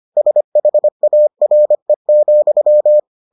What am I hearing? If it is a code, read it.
Morse code, heard as a keyed tone, about 25 wpm, 600 Hz. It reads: SHARE,